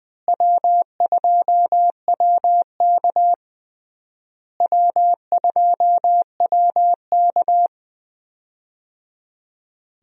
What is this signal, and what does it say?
Morse code, heard as a keyed tone, about 21 wpm, 700 Hz. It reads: W2WK W2WK
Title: USA Call Signs - 4 Characters - Repeated 3x - 20wpm